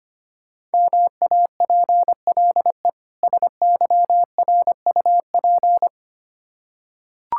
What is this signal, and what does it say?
Morse code, heard as a keyed tone, about 25 wpm, 700 Hz. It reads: MAPLE SYRUP
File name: practice / 25wpm / Difficult Words with Hints - 25wpm